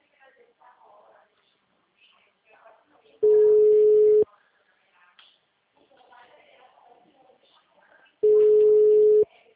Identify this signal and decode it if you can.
Ringback tone (Europe, 425 Hz), with speech.
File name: out- -2041- - - .58333